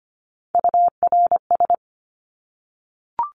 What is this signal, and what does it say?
Morse code, heard as a keyed tone, about 25 wpm, 700 Hz. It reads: URS
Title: Abbreviations Repeated 3x - 25wpm